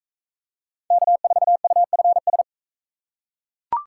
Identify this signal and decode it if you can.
Morse code, heard as a keyed tone, about 44 wpm, 700 Hz. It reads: K4UFS